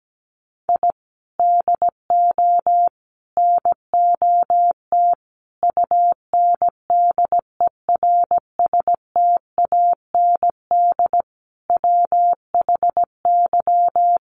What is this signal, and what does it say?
Morse code, heard as a keyed tone, about 17 wpm, 700 Hz. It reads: I DO NOT UNDERSTAND WHY
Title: Sets of 5 Words Encore 17wpm